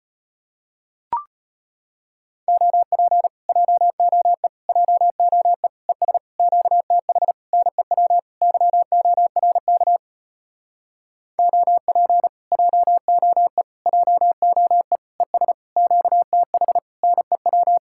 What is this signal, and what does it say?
Morse code, heard as a keyed tone, about 33 wpm, 700 Hz. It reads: OP JOE JOE ES QTH NEW YORK OP JOE JOE ES QTH NEW